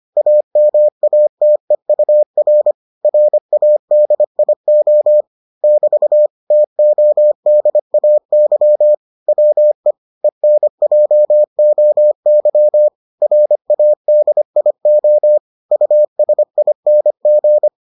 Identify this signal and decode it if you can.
Morse code, heard as a keyed tone, about 25 wpm, 600 Hz. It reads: AMATEUR RADIO <BT> TODAY WE ENJOY RADIO USING